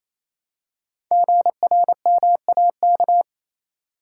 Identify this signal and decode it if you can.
Morse code, heard as a keyed tone, about 27 wpm, 700 Hz. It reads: GRMAK
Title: Code Groups - 5 Letters - 28wpm